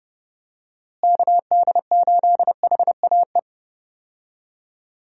Morse code, heard as a keyed tone, about 30 wpm, 700 Hz. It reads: KD8HAE
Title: USA Call Signs - 6 Characters - Repeated 3x - 30wpm